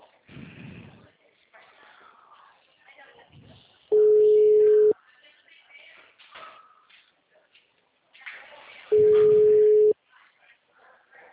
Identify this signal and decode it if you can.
Ringback tone (Europe, 425 Hz), with speech.